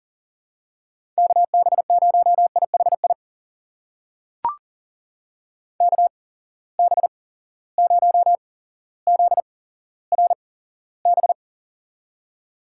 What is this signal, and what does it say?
Morse code, heard as a keyed tone, about 40 wpm, spaced as for 17 wpm, 700 Hz. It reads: KB0IHI KB0ZRB
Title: USA Call Signs - 6 Characters - 8x Character Spacing - 40wpm